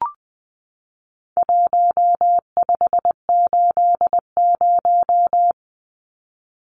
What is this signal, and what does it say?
Morse code, heard as a keyed tone, about 20 wpm, 700 Hz. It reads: 1580
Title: Numbers - Contest Style - 4-Digit 20wpm